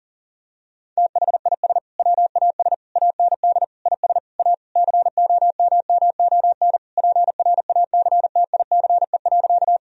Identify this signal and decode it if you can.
Morse code, heard as a keyed tone, about 40 wpm, 700 Hz. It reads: THIS WAS AND IS A COMMON PRACTICE.
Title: Sentences from Top 1000 Words - Review Part 4 - 40wpm